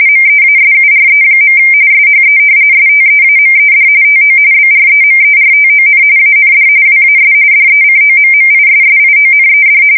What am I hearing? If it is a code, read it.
RTTY, about 45.45 baud. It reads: MEASURE. SHE STEAMED TO BREMERTON AND WAS DECOMMISSIONED
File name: fsk170 / battleshipnj